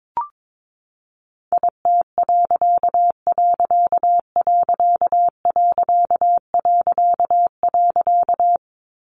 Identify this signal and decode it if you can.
Morse code, heard as a keyed tone, about 22 wpm, 700 Hz. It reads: IT......
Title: A Sound Of Thunder 22wpm